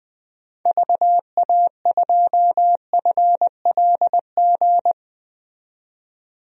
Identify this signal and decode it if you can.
Morse code, heard as a keyed tone, about 20 wpm, 700 Hz. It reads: VA2FLG
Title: DXCC-100 Call Signs - 6 Characters - 20wpm